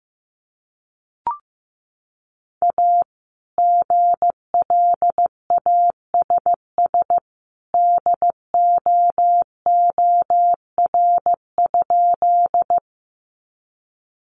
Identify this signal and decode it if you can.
Morse code, heard as a keyed tone, about 15 wpm, 700 Hz. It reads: A GLASS DOOR?
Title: Sentences from Top 1300 Words 15wpm